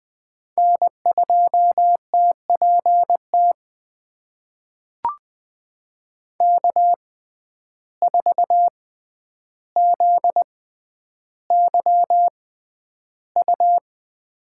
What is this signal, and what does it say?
Morse code, heard as a keyed tone, about 20 wpm, spaced as for 10 wpm, 700 Hz. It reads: N2TPT K4ZYU
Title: USA Call Signs - 5 Characters - 6x Character Spacing - 20wpm